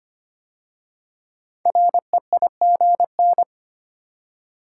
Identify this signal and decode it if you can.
Morse code, heard as a keyed tone, about 25 wpm, 700 Hz. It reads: REIGN
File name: 5 Letter Words - First 3 Letters Given - 25wpm